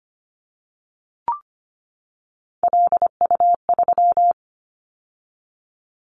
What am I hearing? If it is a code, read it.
Morse code, heard as a keyed tone, about 25 wpm, 700 Hz. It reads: LU3
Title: Callsign Prefixes for Top 100 DXCC - 25wpm